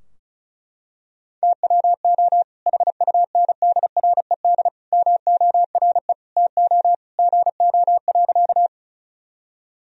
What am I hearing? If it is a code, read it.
Morse code, heard as a keyed tone, about 36 wpm, 700 Hz. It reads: TWO HUNDRED MORE TO GO.
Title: Sentences from Top 1000 Words - Review Part 2 - 35wpm